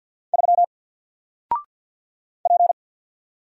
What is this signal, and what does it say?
Morse code, heard as a keyed tone, about 50 wpm, 700 Hz. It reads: 3 P
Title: Single Letter-Number 50wpm